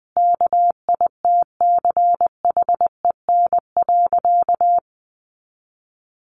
Morse code, heard as a keyed tone, about 20 wpm, 700 Hz. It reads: KITCHEN.
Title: Sentences from Top 1400 Words 20wpm